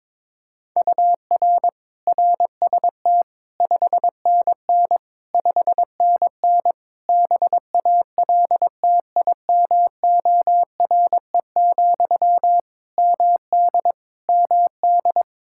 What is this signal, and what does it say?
Morse code, heard as a keyed tone, about 21 wpm, 700 Hz. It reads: UR RST 5NN 5NN BALTIMORE, MD MD
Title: Practice QSO - Exchange 1 - Beginning - 22wpm